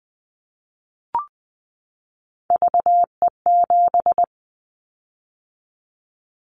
Morse code, heard as a keyed tone, about 20 wpm, 700 Hz. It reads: VE7
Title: Callsign Prefixes for Top 100 DXCC - 20wpm